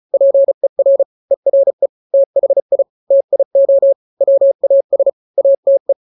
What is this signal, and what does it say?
Morse code, heard as a keyed tone, about 34 wpm, 550 Hz. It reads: PER ERE THI TIO WAS ATE
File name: 3-grams 35 30 WPM